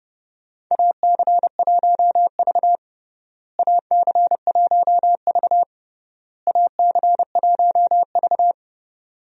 Morse code, heard as a keyed tone, about 30 wpm, 700 Hz. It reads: AC1V AC1V AC1V
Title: USA Call Signs - 4 Characters - Repeated 3x - 30wpm